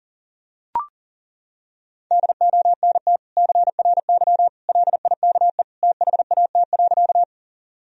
Morse code, heard as a keyed tone, about 37 wpm, 700 Hz. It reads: DONT CRY LIKE THAT.
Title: Sentences from Top 1000 Words - Review Part 2 - 40wpm